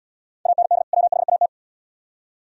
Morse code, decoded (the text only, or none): AND WHAT